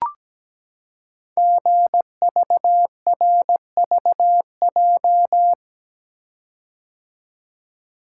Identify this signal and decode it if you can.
Morse code, heard as a keyed tone, about 17 wpm, 700 Hz. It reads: GVRVJ
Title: Code Groups - 5 Letter:Numbers - 17wpm